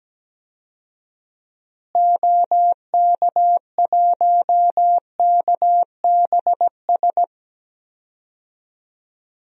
Morse code, heard as a keyed tone, about 17 wpm, 700 Hz. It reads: OK1KBS
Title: DXCC-100 Call Signs - 6 Characters - Repeated 2x - 17wpm